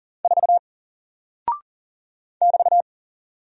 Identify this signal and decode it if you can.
Morse code, heard as a keyed tone, about 40 wpm, 700 Hz. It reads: 4 <BT>
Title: Single Character (Letter:Number:Punctuation:Prosign) Alt 40wpm